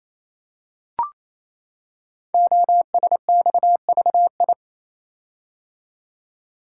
Morse code, heard as a keyed tone, about 28 wpm, 700 Hz. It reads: OSXVI
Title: Code Groups - 5 Letters - 28wpm